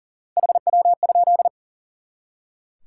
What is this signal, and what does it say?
Morse code, heard as a keyed tone, about 39 wpm, 700 Hz. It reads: HW?
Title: Top 180 QSO Elements 40wpm